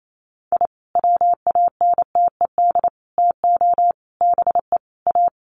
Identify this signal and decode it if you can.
Morse code, heard as a keyed tone, about 28 wpm, 700 Hz. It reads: I WANTED TO BE A
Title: Sets of 5 Words Encore2 28wpm